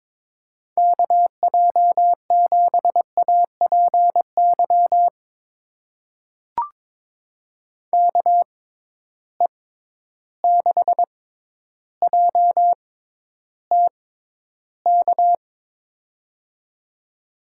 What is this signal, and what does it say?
Morse code, heard as a keyed tone, about 22 wpm, spaced as for 11 wpm, 700 Hz. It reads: KJ7APY KE6JTK
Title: USA Call Signs - 6 Characters - 6x Character Spacing - 22wpm